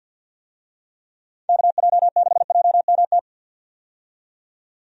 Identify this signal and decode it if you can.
Morse code, heard as a keyed tone, about 52 wpm, 700 Hz. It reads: KJ6JMT
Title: USA Call Signs - 6 Characters - 50wpm